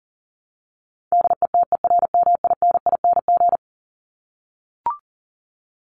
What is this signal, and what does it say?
Morse code, heard as a keyed tone, about 40 wpm, 700 Hz. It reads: DETERMINING
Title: ING Suffix - Top 100 Words with Base Words 40wpm